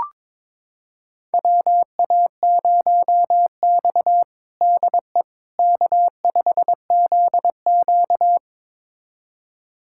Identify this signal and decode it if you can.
Morse code, heard as a keyed tone, about 22 wpm, 700 Hz. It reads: WA0X DE K5ZQ
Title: USA Callsign DE Callsign - 4 Characters - 22wpm